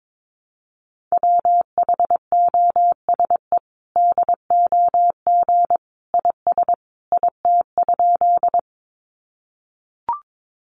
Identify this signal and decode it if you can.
Morse code, heard as a keyed tone, about 22 wpm, 700 Hz. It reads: WHOSE DOG IS IT?
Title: Sentences from Top 1200 Words 22wpm